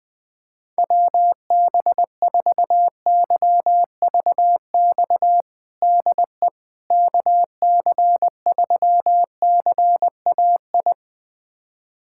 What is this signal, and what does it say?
Morse code, heard as a keyed tone, about 21 wpm, 700 Hz. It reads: WB4YVX DE KC3CAI
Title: USA Callsign DE Callsign - 6 Characters - 20wpm